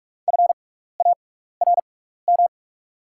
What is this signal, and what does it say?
Morse code, heard as a keyed tone, about 48 wpm, spaced as for 16 wpm, 690 Hz. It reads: FARM
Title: 6x Character Spacing - Top 500 Words - 45wpm